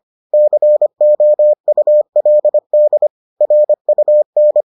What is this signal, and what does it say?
Morse code, heard as a keyed tone, about 25 wpm, 600 Hz. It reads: COULD RUN